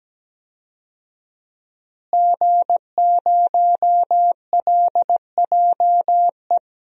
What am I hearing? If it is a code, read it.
Morse code, heard as a keyed tone, about 17 wpm, 700 Hz. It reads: G0LJE
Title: DXCC-100 Call Signs - 5 Characters - Repeated 2x - 17wpm